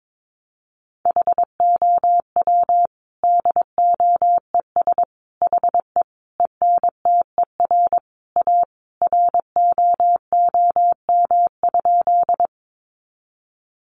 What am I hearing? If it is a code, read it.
Morse code, heard as a keyed tone, about 22 wpm, 700 Hz. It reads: HOW DOES HE ENTER A ROOM?
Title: Sentences from Top 1300 Words 22wpm